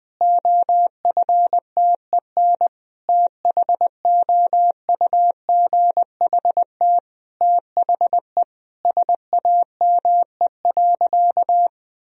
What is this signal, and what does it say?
Morse code, heard as a keyed tone, about 20 wpm, 700 Hz. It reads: OFTEN THOUGHT THE SAME.
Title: Sentences from Top 1000 Words - Review Part 1 - 20wpm